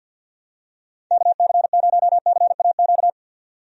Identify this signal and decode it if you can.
Morse code, heard as a keyed tone, about 51 wpm, 700 Hz. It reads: KC0CAQ
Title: USA Call Signs - 6 Characters - 50wpm